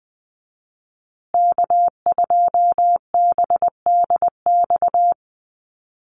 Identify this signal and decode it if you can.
Morse code, heard as a keyed tone, about 20 wpm, 700 Hz. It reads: K2BDX
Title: USA Call Signs Repeated 2x 20wpm